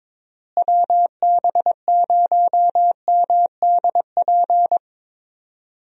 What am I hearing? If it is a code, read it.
Morse code, heard as a keyed tone, about 22 wpm, 700 Hz. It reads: WB0MDP
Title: USA Call Signs - 6 Characters - Repeated 3x - 22wpm